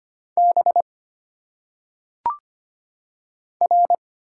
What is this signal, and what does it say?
Morse code, heard as a keyed tone, about 25 wpm, 700 Hz. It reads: B R
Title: Single Letters - Phonetic Pronunciation 25wpm